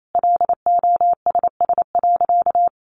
Morse code, heard as a keyed tone, about 28 wpm, 700 Hz. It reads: LOSS.
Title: Sentences from Top 1400 Words 28wpm